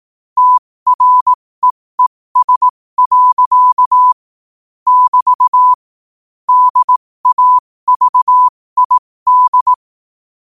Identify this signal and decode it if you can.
Morse code, heard as a keyed tone, about 18 wpm, spaced as for 15 wpm, 1000 Hz. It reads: TREES. <BT> DAVID
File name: qotdfiles / 15wpm